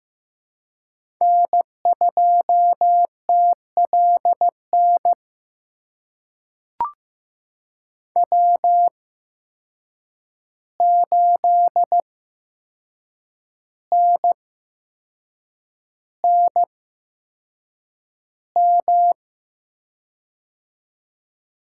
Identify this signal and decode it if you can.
Morse code, heard as a keyed tone, about 15 wpm, spaced as for 6 wpm, 700 Hz. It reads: N2TLN W8NNM
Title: USA Call Signs - 5 Characters - 8x Character Spacing - 15wpm